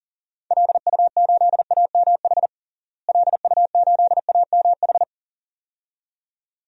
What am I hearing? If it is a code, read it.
Morse code, heard as a keyed tone, about 40 wpm, 700 Hz. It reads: LU8AMH LU8AMH
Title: DXCC-100 Call Signs - 6 Characters - Repeated 2x - 40wpm